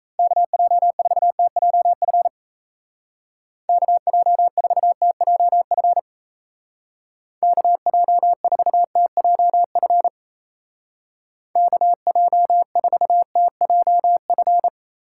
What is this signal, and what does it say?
Morse code, heard as a keyed tone, about 36 wpm, 700 Hz. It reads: KJ4TJF KJ4TJF KJ4TJF KJ4TJF